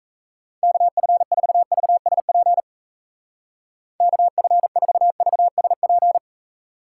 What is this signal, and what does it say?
Morse code, heard as a keyed tone, about 36 wpm, 700 Hz. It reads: KF4VSP KF4VSP